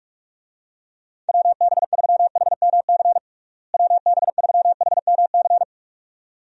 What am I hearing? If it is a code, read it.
Morse code, heard as a keyed tone, about 44 wpm, 700 Hz. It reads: WB3HMC WB3HMC
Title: USA Call Signs - 6 Characters - Repeated 2x - 45wpm